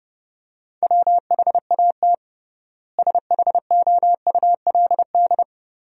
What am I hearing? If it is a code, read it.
Morse code, heard as a keyed tone, about 30 wpm, 700 Hz. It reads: WHAT SHOULD
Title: 3x Word-Spacing - Sentences from Top 300 Words - 30wpm